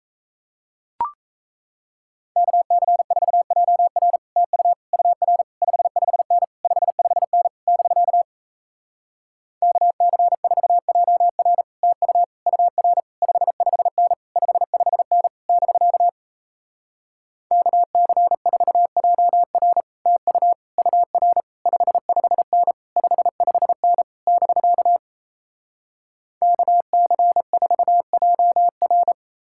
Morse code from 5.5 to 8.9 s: 55N 55N <BK>

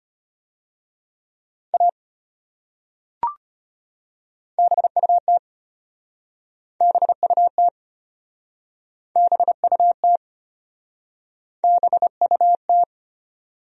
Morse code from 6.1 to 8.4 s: BUT

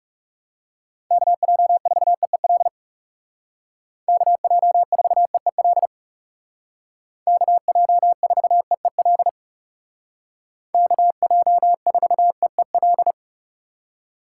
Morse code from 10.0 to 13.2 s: KJ4EEL

